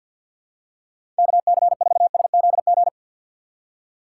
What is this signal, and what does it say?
Morse code, heard as a keyed tone, about 53 wpm, 700 Hz. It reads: KC4SZG
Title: USA Call Signs - 6 Characters - 8x Character Spacing - 50wpm